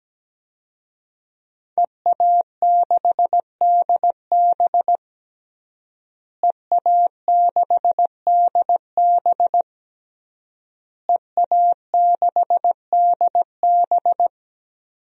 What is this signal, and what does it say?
Morse code, heard as a keyed tone, about 17 wpm, 700 Hz. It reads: EA6DB EA6DB EA6DB
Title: DXCC-100 Call Signs - 5 Characters - Repeated 3x - 17wpm